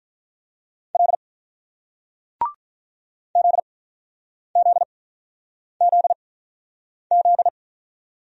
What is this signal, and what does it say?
Morse code, heard as a keyed tone, about 47 wpm, spaced as for 9 wpm, 700 Hz. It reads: L ZZZZ